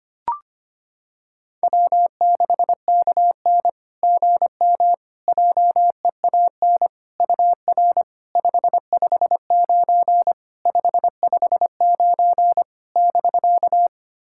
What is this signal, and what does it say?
Morse code, heard as a keyed tone, about 25 wpm, 700 Hz. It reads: W6KN GM JEAN UR 559 559 <BK>
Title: SOTA QSO Practice 25wpm